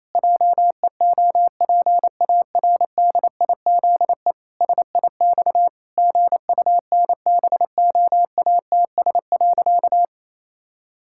Morse code, heard as a keyed tone, about 27 wpm, 700 Hz. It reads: JEOPARDIZE SIX GUNBOATS.
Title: Pangram Challenge v2 - 28wpm